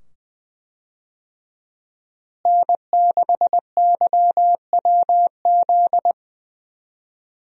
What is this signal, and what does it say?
Morse code, heard as a keyed tone, about 20 wpm, 700 Hz. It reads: N6YWZ
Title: USA Call Signs - 5 Characters - Repeated 2x - 20wpm